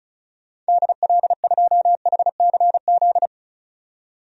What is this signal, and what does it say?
Morse code, heard as a keyed tone, about 34 wpm, 700 Hz. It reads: DL2HCZ
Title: DXCC-100 Call Signs - 6 Characters - 35wpm